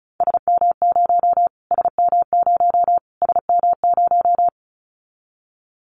Morse code, heard as a keyed tone, about 34 wpm, 700 Hz. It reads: SM0 SM0 SM0
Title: Callsign Prefixes for Top 25 DXCC - Repeated 3x - 35wpm